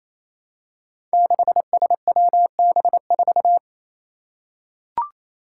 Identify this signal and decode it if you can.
Morse code, heard as a keyed tone, about 28 wpm, 700 Hz. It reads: 6SWB4
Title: Five Challenging Letter-Number Combinations 28wpm